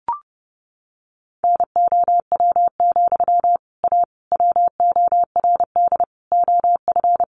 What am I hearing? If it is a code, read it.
Morse code, heard as a keyed tone, about 28 wpm, 700 Hz. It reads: NOW, A WORD OF
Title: Sentences from Top 1100 Words 30wpm